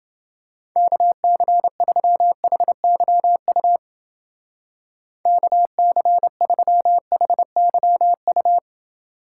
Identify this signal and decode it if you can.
Morse code, heard as a keyed tone, about 29 wpm, 700 Hz. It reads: KC3HYU KC3HYU